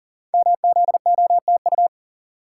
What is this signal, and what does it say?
Morse code, heard as a keyed tone, about 41 wpm, 700 Hz. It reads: MZOTU
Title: Code Groups - 5 Letters - 40wpm